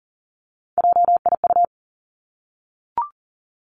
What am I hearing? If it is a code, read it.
Morse code, heard as a keyed tone, about 38 wpm, 700 Hz. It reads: JIU
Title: Three Challenging Letter-Number Combinations 40wpm